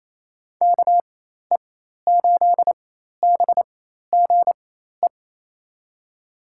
Morse code, heard as a keyed tone, about 28 wpm, spaced as for 13 wpm, 690 Hz. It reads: KE8BGE
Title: USA Call Signs - 6 Characters - 4x Character Spacing - 28wpm